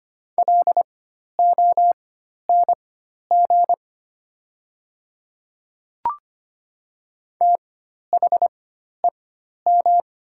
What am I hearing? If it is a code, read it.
Morse code, heard as a keyed tone, about 25 wpm, spaced as for 12 wpm, 700 Hz. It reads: LONG THEM
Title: 4x Character Spacing - Top 100 Words - 25wpm